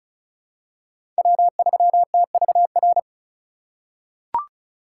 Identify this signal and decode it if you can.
Morse code, heard as a keyed tone, about 36 wpm, 700 Hz. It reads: W3TVR
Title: USA Call Signs - 5 Characters - 4x Character Spacing - 35wpm